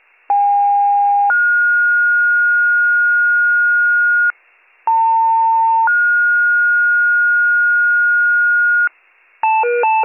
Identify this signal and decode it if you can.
Two-tone page, heard as a one-second tone, then a three-second tone, about 810 then 1430 Hz, about 900 then 1430 Hz. It is heard from 0.3 to 8.9 s.